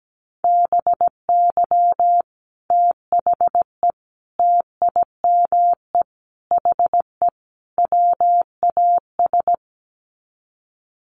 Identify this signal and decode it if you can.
Morse code, heard as a keyed tone, about 17 wpm, 700 Hz. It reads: BY THE TIME HE WAS
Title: Sets of 5 Words Encore 17wpm